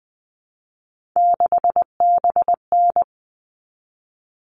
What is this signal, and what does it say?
Morse code, heard as a keyed tone, about 20 wpm, 700 Hz. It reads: 6BN
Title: All Cut Numbers - 3-Digit 20wpm